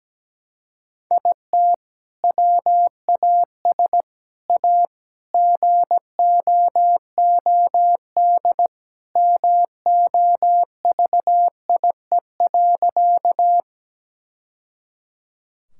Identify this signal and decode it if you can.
Morse code, heard as a keyed tone, about 17 wpm, 700 Hz. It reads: IT WAS A GOOD MOVIE.
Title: Sentences from Top 1000 Words - Review Part 5 - 17wpm